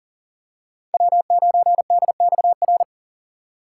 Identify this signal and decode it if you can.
Morse code, heard as a keyed tone, about 40 wpm, 700 Hz. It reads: W9DXR
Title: K1USN SST QSO Practice - 40wpm